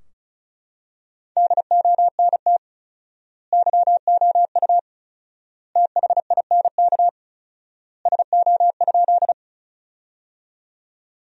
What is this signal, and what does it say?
Morse code, heard as a keyed tone, about 34 wpm, 700 Hz. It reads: DONT YOU THINK SO?
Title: 4x Word-Spacing - Sentences from Top 300 Words - 35wpm